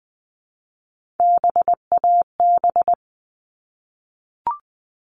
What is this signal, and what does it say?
Morse code, heard as a keyed tone, about 20 wpm, 700 Hz. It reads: BAB
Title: All Cut Numbers - 3-Digit 20wpm